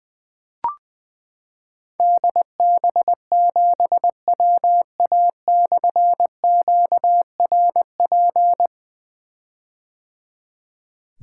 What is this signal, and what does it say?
Morse code, heard as a keyed tone, about 20 wpm, 700 Hz. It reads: DB7WA/QRP
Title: Difficult Call Signs 20wpm